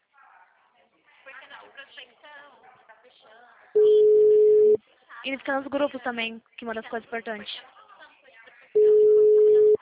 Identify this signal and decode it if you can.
Ringback tone (Europe, 425 Hz), with speech.
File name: out- -2017- - - .96188